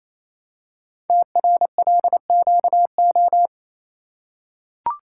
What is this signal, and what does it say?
Morse code, heard as a keyed tone, about 28 wpm, 700 Hz. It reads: TRLQO
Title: Code Groups - 5 Letter:Numbers - 28wpm